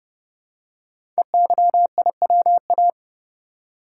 Morse code, heard as a keyed tone, about 30 wpm, 700 Hz. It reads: EYIWA
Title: Code Groups - 5 Letters - 30wpm